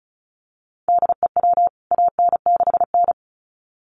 Morse code, heard as a keyed tone, about 34 wpm, 700 Hz. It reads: DEW AN6N
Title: CWT Exchange 35wpm